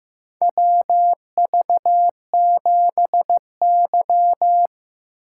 Morse code, heard as a keyed tone, about 15 wpm, 700 Hz. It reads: WV7Y
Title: USA Call Signs - 4x Character Spacing - 15wpm